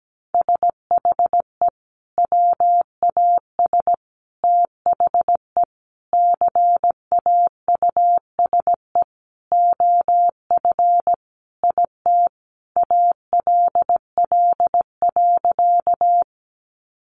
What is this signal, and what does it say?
Morse code, heard as a keyed tone, about 17 wpm, 700 Hz. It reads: SHE WAS THE CAUSE OF IT ALL.
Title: Sentences from Top 200 Words 17wpm